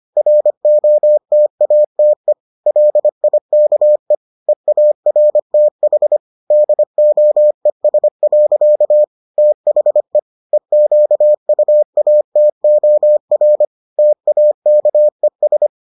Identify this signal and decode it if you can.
Morse code, heard as a keyed tone, about 25 wpm, 600 Hz. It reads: ROTATE LIKE EARTH DOES. THE EQUATOR TAKES